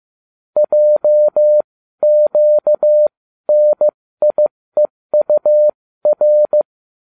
Morse code, heard as a keyed tone, about 15 wpm, spaced as for 13 wpm, 600 Hz. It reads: JQNIEUR